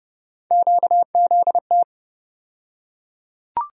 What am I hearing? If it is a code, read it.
Morse code, heard as a keyed tone, about 30 wpm, 700 Hz. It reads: QZT
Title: USA Call Sign Postfixes - 3 Letters 30wpm